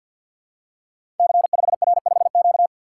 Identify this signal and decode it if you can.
Morse code, heard as a keyed tone, about 48 wpm, 700 Hz. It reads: C5R5Q
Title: Code Groups - 5 Letter:Numbers - 50wpm